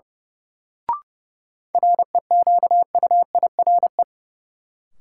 Morse code, heard as a keyed tone, about 32 wpm, 700 Hz. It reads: REQUIRE
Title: Sentences from Top 900 Words - New Word Drill - 30wpm